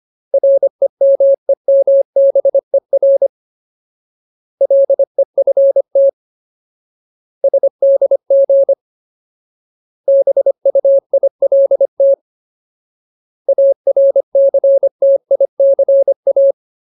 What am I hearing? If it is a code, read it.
Morse code, heard as a keyed tone, about 25 wpm, 550 Hz. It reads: REMEMBER LEFT SDG BUILT ARCTICA
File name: Ultimate Word List / 25WPM Ultimate